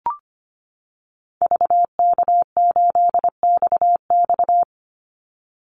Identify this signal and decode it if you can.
Morse code, heard as a keyed tone, about 25 wpm, 700 Hz. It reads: VK8XX
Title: Call Signs - 25wpm